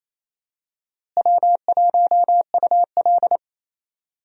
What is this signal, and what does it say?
Morse code, heard as a keyed tone, about 28 wpm, 700 Hz. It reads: W1UL
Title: USA Call Signs - 4 Characters - Repeated 3x - 28wpm